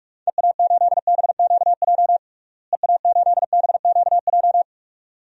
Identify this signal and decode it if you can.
Morse code, heard as a keyed tone, about 46 wpm, 700 Hz. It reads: EA8BQJ EA8BQJ
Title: DXCC-100 Call Signs - 6 Characters - Repeated 2x - 45wpm